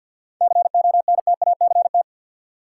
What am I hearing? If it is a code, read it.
Morse code, heard as a keyed tone, about 50 wpm, 700 Hz. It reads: CONTACT
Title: Seven Letter Words - 50wpm